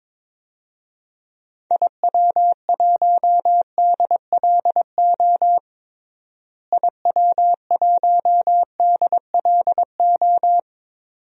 Morse code, heard as a keyed tone, about 22 wpm, 700 Hz. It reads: IW1DLO IW1DLO
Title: DXCC-100 Call Signs - 6 Characters - Repeated 2x - 22wpm